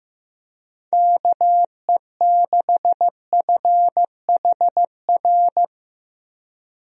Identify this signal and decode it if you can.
Morse code, heard as a keyed tone, about 15 wpm, 700 Hz. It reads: KE6FHR
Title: USA Call Signs - 6 Characters - Repeated 2x - 15wpm